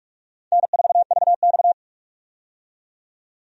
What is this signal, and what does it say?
Morse code, heard as a keyed tone, about 44 wpm, 700 Hz. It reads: N4VX
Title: USA Call Signs - 4 Characters - 45wpm